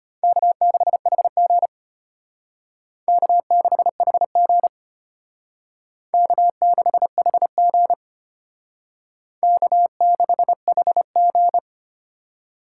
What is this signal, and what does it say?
Morse code, heard as a keyed tone, about 32 wpm, 700 Hz. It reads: K6HG K6HG K6HG K6HG